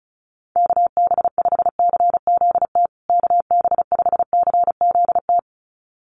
Morse code, heard as a keyed tone, about 34 wpm, 700 Hz. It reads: KB5CZT KB5CZT
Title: USA Call Signs Repeated 2x 35wpm